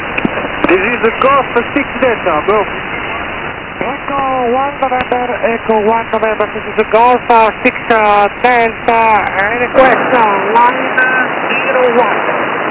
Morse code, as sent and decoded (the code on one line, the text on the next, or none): none